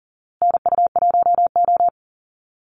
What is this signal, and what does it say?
Morse code, heard as a keyed tone, about 40 wpm, 700 Hz. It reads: NU1O
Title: Call Signs Repeated 2x - 40wpm